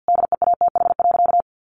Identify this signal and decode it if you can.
Morse code, heard as a keyed tone, about 50 wpm, 700 Hz. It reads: DEATH.
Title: Sentences from Top 1300 Words 50wpm